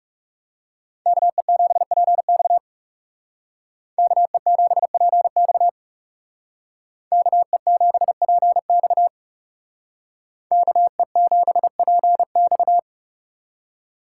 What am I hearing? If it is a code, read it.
Morse code, heard as a keyed tone, about 40 wpm, 700 Hz. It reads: KE7PX KE7PX KE7PX KE7PX